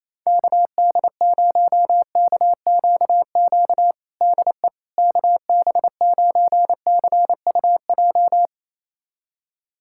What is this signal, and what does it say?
Morse code, heard as a keyed tone, about 27 wpm, 700 Hz. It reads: KD0KQQ DE KB9CUJ
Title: USA Callsign DE Callsign - 6 Characters - 28wpm